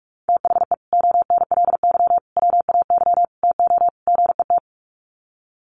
Morse code, heard as a keyed tone, about 48 wpm, 700 Hz. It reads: THE ONLY WAY TO GET
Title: Sets of 5 Words Encore2 45wpm